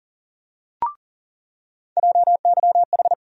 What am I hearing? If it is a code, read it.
Morse code, heard as a keyed tone, about 40 wpm, 700 Hz. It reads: JYH
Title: USA Call Sign Postfixes - 3 Letters 40wpm